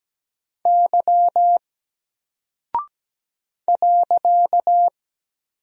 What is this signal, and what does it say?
Morse code, heard as a keyed tone, about 17 wpm, 700 Hz. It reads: Y .
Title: Single Character (Letter:Number:Punctuation:Prosign) 17wpm